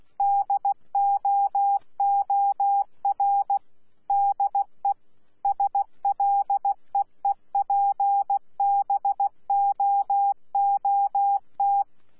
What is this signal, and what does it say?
Morse code, heard as a keyed tone, about 16 wpm, 800 Hz. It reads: DOOR DE SLEEPBOOT
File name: Z PI4VRZ / PI4VRZ 14-jun-2025